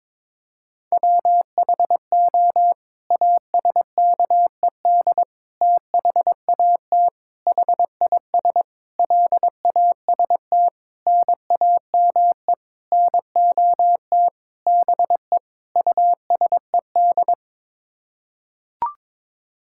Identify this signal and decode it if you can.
Morse code, heard as a keyed tone, about 22 wpm, 700 Hz. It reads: WHO ASKED THAT HIS LAST NAME NOT BE USED
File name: Sets of 9 Words - 22wpm